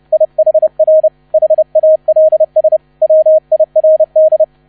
Morse code, heard as a keyed tone, about 30 wpm, 630 Hz, with radio noise. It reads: IHR HALS WIRD